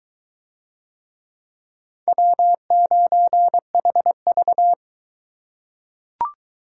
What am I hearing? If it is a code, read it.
Morse code, heard as a keyed tone, about 22 wpm, 700 Hz. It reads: W9HV